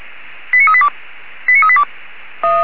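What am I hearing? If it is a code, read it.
Five-tone selective call (ZVEI), heard as quick runs of five tones, tones of about 70 ms. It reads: 89382 89382
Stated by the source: ZVEI five-tone selective call received by SDR.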